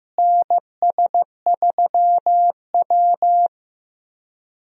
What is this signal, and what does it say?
Morse code, heard as a keyed tone, about 15 wpm, 700 Hz. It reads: NS3W
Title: USA Call Signs - 4 Characters - Repeated 3x - 15wpm